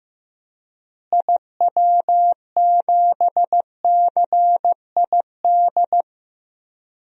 Morse code, heard as a keyed tone, about 15 wpm, 700 Hz. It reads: IW7CID
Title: DXCC-100 Call Signs - 6 Characters - Repeated 2x - 15wpm